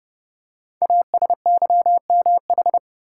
Morse code, heard as a keyed tone, about 30 wpm, 690 Hz. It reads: ASYMH
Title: Code Groups - 5 Letters - 30wpm